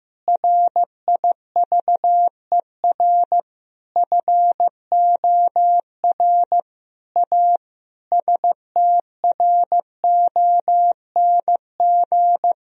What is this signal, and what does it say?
Morse code, heard as a keyed tone, about 15 wpm, 700 Hz. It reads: RIVER FOR A STRONG
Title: Sentences from Top 1000 Words - Review Part 4 - 15wpm